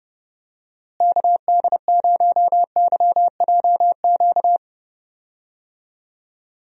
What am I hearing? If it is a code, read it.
Morse code, heard as a keyed tone, about 31 wpm, 700 Hz. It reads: KD0YJQ
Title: USA Call Signs - 6 Characters - Repeated 3x - 30wpm